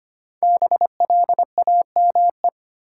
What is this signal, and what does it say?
Morse code, heard as a keyed tone, about 25 wpm, 700 Hz. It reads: BLAME
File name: 5 Letter Words - First 3 Letters Given - 25wpm